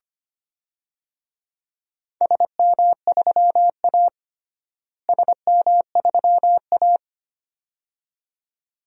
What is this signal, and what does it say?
Morse code, heard as a keyed tone, about 25 wpm, 700 Hz. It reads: SM3A SM3A
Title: DXCC-100 Call Signs - 4 Characters - Repeated 2x - 25wpm